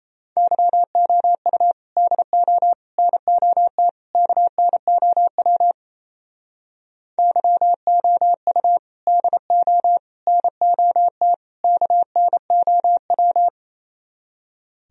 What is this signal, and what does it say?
Morse code, heard as a keyed tone, about 34 wpm, 700 Hz. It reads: YOU DO NOT KNOW YOU DO NOT KNOW
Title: Speed-Racing - Sets of 4 Words 28wpm